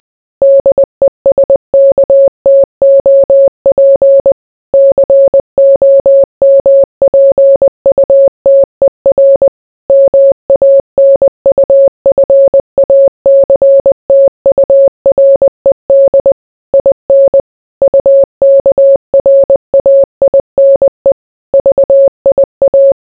Morse code, heard as a keyed tone, about 20 wpm, 550 Hz. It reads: DESKTOP COMPUTER MANUFACTURED IN UKRAINE VIA